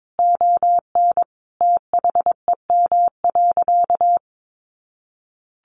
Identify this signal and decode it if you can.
Morse code, heard as a keyed tone, about 22 wpm, 700 Hz. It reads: ON THEM.
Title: Sentences from Top 1300 Words 22wpm